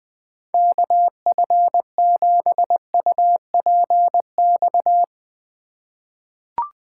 Morse code, heard as a keyed tone, about 20 wpm, 700 Hz. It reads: KF7UPX
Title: USA Call Signs - 6 Characters - 3x Character Spacing - 20wpm